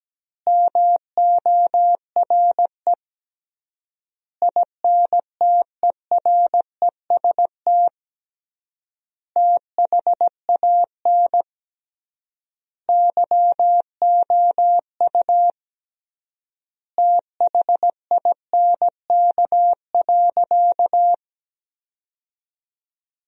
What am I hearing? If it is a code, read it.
Morse code, heard as a keyed tone, about 17 wpm, 700 Hz. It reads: MORE INTEREST THAN YOU THINK.
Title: 3x Word-Spacing - Sentences from Top 400 Words - 17wpm